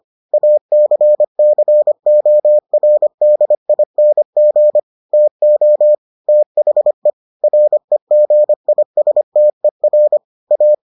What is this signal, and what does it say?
Morse code, heard as a keyed tone, about 25 wpm, 600 Hz. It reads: ACCORDING TO THE REGISTER A